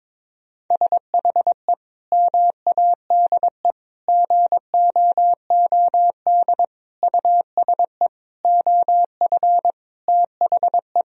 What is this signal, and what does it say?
Morse code, heard as a keyed tone, about 22 wpm, 700 Hz. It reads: SHE MADE GOOD USE OF THE
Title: Sentences from Top 1000 Words - Review Part 4 - 22wpm